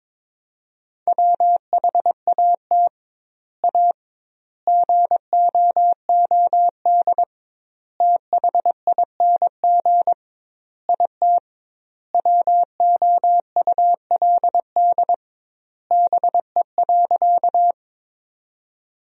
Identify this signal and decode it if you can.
Morse code, heard as a keyed tone, about 21 wpm, 700 Hz. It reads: WHAT A GOOD THING IT WOULD BE.
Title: 2x Word-Spacing - Sentences from Top 200 Words - 22wpm